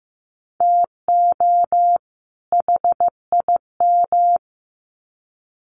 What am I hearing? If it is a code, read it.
Morse code, heard as a keyed tone, about 15 wpm, 700 Hz. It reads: TO HIM
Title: Sets of 2 Words 15wpm